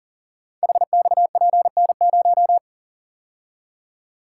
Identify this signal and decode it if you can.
Morse code, heard as a keyed tone, about 38 wpm, 700 Hz. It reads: HXPN0
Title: Code Groups - 5 Letter:Numbers - 40wpm